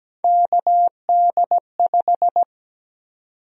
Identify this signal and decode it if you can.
Morse code, heard as a keyed tone, about 17 wpm, 700 Hz. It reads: KD5
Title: USA Call Sign Prefixes - 17wpm